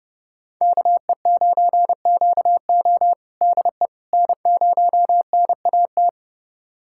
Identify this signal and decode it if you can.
Morse code, heard as a keyed tone, about 30 wpm, 700 Hz. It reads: KE9QO DE N0NAT
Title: USA Callsign DE Callsign - 5 Characters - 30wpm